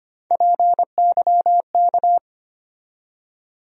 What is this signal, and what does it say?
Morse code, heard as a keyed tone, about 25 wpm, 700 Hz. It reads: PYK
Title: USA Call Sign Postfixes - 3 Letters 25wpm